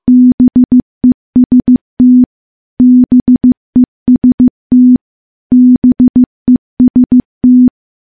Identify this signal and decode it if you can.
Morse code, heard as a keyed tone, about 15 wpm, 250 Hz. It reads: BEST BEST BEST
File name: morse / morse-training-15WPM